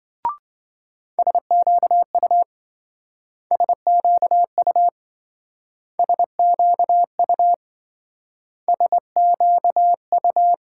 Morse code, heard as a keyed tone, about 24 wpm, 700 Hz. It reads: SQU SQU SQU SQU
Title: Speed-Racing Multi-letter Phonemes - 20wpm